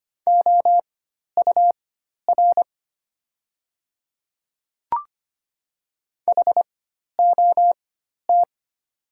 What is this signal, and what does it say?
Morse code, heard as a keyed tone, about 25 wpm, spaced as for 12 wpm, 700 Hz. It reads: OUR HOT
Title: 4x Character Spacing - Top 300 Words - 25wpm